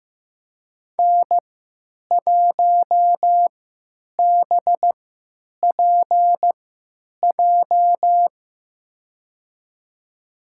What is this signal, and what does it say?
Morse code, heard as a keyed tone, about 15 wpm, spaced as for 9 wpm, 700 Hz. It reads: N1BPJ
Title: USA Call Signs - 3x Character Spacing - 15wpm